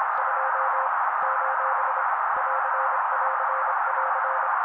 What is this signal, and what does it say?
Morse code, heard as a keyed tone, about 26 wpm, 530 Hz. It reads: 2 : <AR><AR><AR>
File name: daily / 25wpm noise